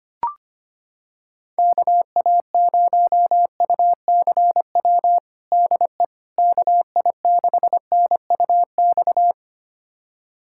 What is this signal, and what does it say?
Morse code, heard as a keyed tone, about 25 wpm, 700 Hz. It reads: KA0UCW DE KI6NUX
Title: USA Callsign DE Callsign - 6 Characters - 25wpm